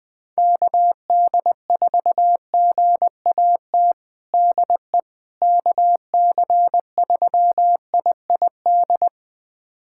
Morse code, heard as a keyed tone, about 20 wpm, 700 Hz. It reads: KD4GAT DE KC3IID
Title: USA Callsign DE Callsign - 6 Characters - 20wpm